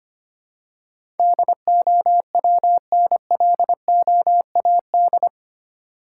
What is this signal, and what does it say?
Morse code, heard as a keyed tone, about 25 wpm, 700 Hz. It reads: DOWNLOAD
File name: Eight Letter Words - 25wpm